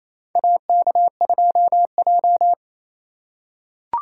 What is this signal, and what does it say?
Morse code, heard as a keyed tone, about 28 wpm, 700 Hz. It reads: AK2J
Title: USA Call Signs - 4x Character Spacing - 28wpm